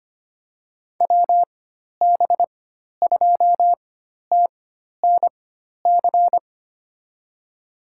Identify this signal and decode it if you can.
Morse code, heard as a keyed tone, about 25 wpm, spaced as for 12 wpm, 700 Hz. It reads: WB2TNC
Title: USA Call Signs - 4x Character Spacing - 25wpm